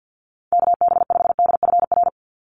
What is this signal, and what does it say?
Morse code, heard as a keyed tone, about 50 wpm, 700 Hz. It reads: KB5DFR
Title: USA Call Signs 50wpm